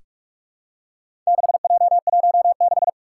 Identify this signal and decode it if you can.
Morse code, heard as a keyed tone, about 44 wpm, 700 Hz. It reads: 6J16
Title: Four Challenging Letter-Number Combinations 45wpm